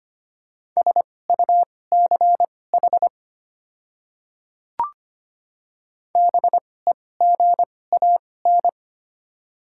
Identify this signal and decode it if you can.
Morse code, heard as a keyed tone, about 25 wpm, spaced as for 18 wpm, 690 Hz. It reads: SUCH BEGAN
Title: 2x Character Spacing - Top 300 Words - 25wpm